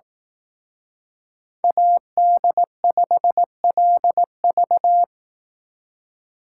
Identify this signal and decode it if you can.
Morse code, heard as a keyed tone, about 18 wpm, 700 Hz. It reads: AD5LV
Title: Speed Racing - USA Call Signs 15wpm